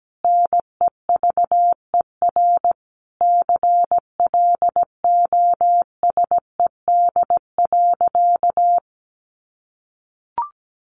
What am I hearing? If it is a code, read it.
Morse code, heard as a keyed tone, about 17 wpm, 700 Hz. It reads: NEVER CLOSED.
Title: Sentences from Top 1400 Words 17wpm